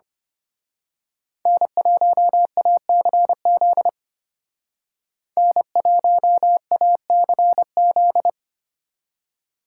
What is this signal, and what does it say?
Morse code, heard as a keyed tone, about 28 wpm, 700 Hz. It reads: N1ACZ N1ACZ
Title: Speed Racing - USA 1x3 Call Signs 25wpm